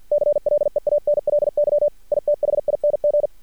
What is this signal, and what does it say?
Morse code, heard as a keyed tone, about 50 wpm, 590 Hz. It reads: CLEANLY ITHING
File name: archived / CRSnet